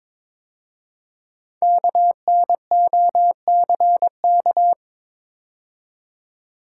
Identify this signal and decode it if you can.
Morse code, heard as a keyed tone, about 22 wpm, 700 Hz. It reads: KNOCK
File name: 5 Letter Words - First 3 Letters Given - 22wpm